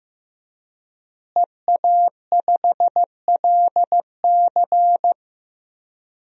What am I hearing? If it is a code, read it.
Morse code, heard as a keyed tone, about 15 wpm, 700 Hz. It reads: EA5LC
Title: DXCC-100 Call Signs - 5 Characters - Repeated 2x - 15wpm